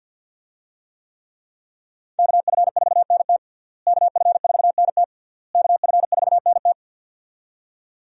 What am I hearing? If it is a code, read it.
Morse code, heard as a keyed tone, about 53 wpm, 700 Hz. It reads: KF4NT KF4NT KF4NT
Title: USA Call Signs - 5 Characters - Repeated 3x - 50wpm